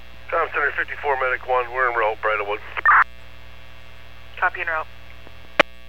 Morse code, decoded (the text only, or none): none